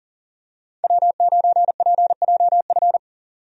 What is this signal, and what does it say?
Morse code, heard as a keyed tone, about 40 wpm, 700 Hz. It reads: W9PJF
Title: USA Call Signs - 5 Characters - 40wpm